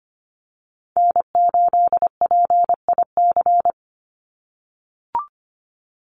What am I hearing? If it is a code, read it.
Morse code, heard as a keyed tone, about 25 wpm, 700 Hz. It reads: N8PIC
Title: USA Call Signs 25wpm